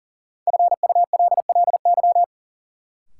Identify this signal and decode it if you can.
Morse code, heard as a keyed tone, about 40 wpm, 700 Hz. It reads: FULLY
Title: Sentences from Top 1300 Words - New Word Drill Alt - 40wpm